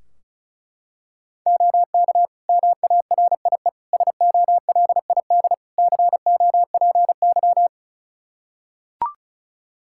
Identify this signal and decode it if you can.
Morse code, heard as a keyed tone, about 36 wpm, 700 Hz. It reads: OK MARIE SOLID COPY
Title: Practice QSO - Exchange 3 - More Info - 35wpm